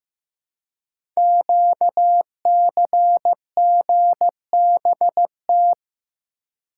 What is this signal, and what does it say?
Morse code, heard as a keyed tone, about 15 wpm, 700 Hz. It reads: QCGBT
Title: Code Groups - 5 Letter:Numbers - 15wpm